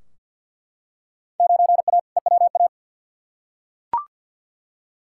Morse code, heard as a keyed tone, about 46 wpm, 700 Hz. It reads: 9A EWA
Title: Field Day Exchanges Encore - 50wpm